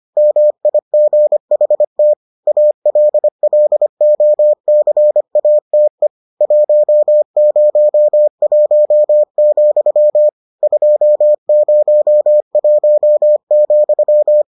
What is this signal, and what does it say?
Morse code, heard as a keyed tone, about 25 wpm, 600 Hz. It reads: MIGHT ALLOCATE 101, 201,